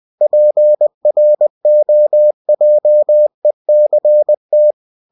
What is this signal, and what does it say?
Morse code, heard as a keyed tone, about 20 wpm, 600 Hz. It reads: PROJECT
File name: cw-books / winnie-the-pooh / winnie-20wpm-page-0115